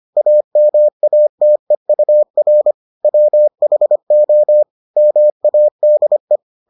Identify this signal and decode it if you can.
Morse code, heard as a keyed tone, about 25 wpm, 600 Hz. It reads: AMATEUR WHO MADE